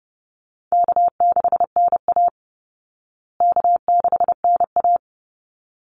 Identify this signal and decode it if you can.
Morse code, heard as a keyed tone, about 30 wpm, 700 Hz. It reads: K6NA K6NA
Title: Easy Call Signs Repeated 2x - 30wpm